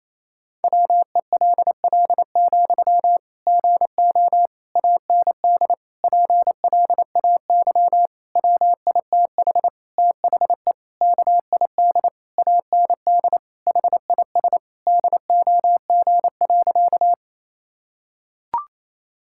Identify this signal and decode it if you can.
Morse code, heard as a keyed tone, about 28 wpm, 700 Hz. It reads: WELL, GO AND PLAY WITH THE KID AND HIS DOG.
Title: Sentences from Top 1600 Words 28wpm